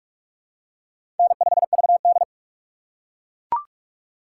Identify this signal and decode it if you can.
Morse code, heard as a keyed tone, about 42 wpm, 700 Hz. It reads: N5VD
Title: USA Call Signs - 4 Characters - 4x Character Spacing - 45wpm